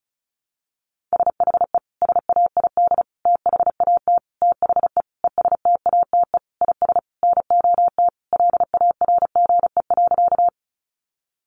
Morse code, heard as a keyed tone, about 34 wpm, 700 Hz. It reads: SHE SAID THAT THE ESTATE IS NOT LARGE.
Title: Sentences from Top 1200 Words 35wpm